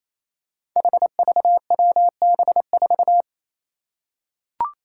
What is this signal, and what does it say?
Morse code, heard as a keyed tone, about 27 wpm, 700 Hz. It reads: HVWB4
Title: Five Challenging Letter-Number Combinations 28wpm